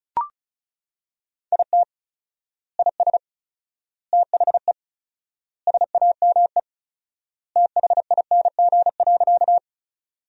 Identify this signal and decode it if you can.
Morse code, heard as a keyed tone, about 36 wpm, 700 Hz. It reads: IT IS THE SAME THING.
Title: 4x Word-Spacing - Sentences from Top 200 Words - 35wpm